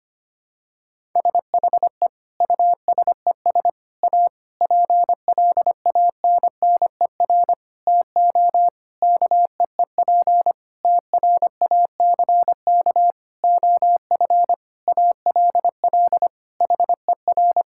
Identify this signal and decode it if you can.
Morse code, heard as a keyed tone, about 25 wpm, 700 Hz. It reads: SHE USES A PLANNER TO KEEP TRACK OF ALL HER
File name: practice / 25wpm / Difficult Sentences with Hints - 25wpm